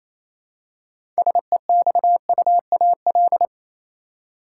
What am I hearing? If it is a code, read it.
Morse code, heard as a keyed tone, about 27 wpm, 700 Hz. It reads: SEXUAL
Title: Six Letter Words - 28wpm